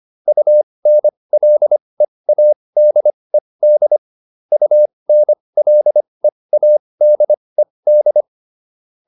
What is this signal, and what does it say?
Morse code, heard as a keyed tone, about 25 wpm, spaced as for 20 wpm, 600 Hz. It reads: UNLEADED UNLEADED